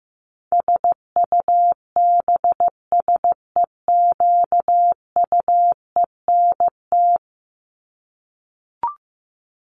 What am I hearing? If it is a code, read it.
Morse code, heard as a keyed tone, about 15 wpm, 700 Hz. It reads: SUBSEQUENT
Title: LY Suffix - Top 100 Words with Base Words 15wpm